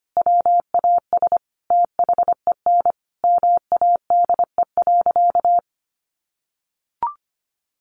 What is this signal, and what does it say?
Morse code, heard as a keyed tone, about 25 wpm, 700 Hz. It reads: WAS THEN MADE.
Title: Sentences from Top 1500 Words 25wpm